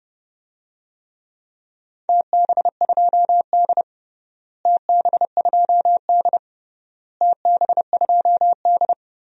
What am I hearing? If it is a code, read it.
Morse code, heard as a keyed tone, about 30 wpm, 700 Hz. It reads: TB2D TB2D TB2D
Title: DXCC-100 Call Signs - 4 Characters - Repeated 3x - 30wpm